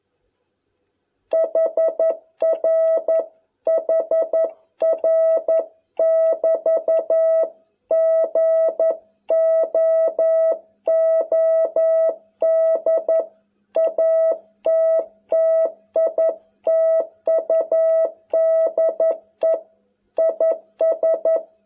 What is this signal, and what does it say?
Morse code, heard as a keyed tone, about 11 wpm, 660 Hz. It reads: HR HR <BT> GOOD ATTITUDE IS